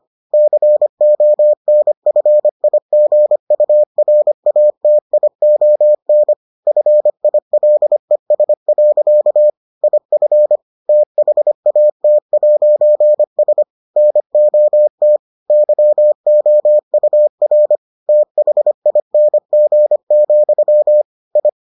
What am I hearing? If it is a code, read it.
Morse code, heard as a keyed tone, about 25 wpm, 600 Hz. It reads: CONFIGURATION FILES. IF THAT'S NOT YOUR THING, I